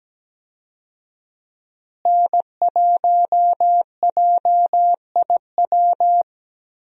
Morse code, heard as a keyed tone, about 17 wpm, 700 Hz. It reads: N1JIW
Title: USA Call Signs - 5 Characters - Repeated 3x - 17wpm